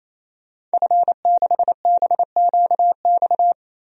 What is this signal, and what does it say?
Morse code, heard as a keyed tone, about 28 wpm, 700 Hz. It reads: F6BQX
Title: DXCC-100 Call Signs - 5 Characters - 28wpm